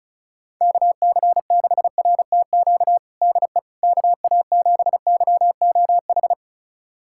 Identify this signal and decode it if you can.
Morse code, heard as a keyed tone, about 36 wpm, 700 Hz. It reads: KC6RTQ DE KA7YOH
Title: USA Callsign DE Callsign - 6 Characters - 35wpm